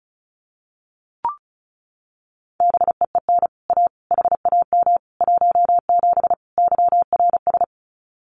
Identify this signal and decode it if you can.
Morse code, heard as a keyed tone, about 34 wpm, 700 Hz. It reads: BEEN A HAM 17 YRS
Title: Common QSO Phrases Encore 35wpm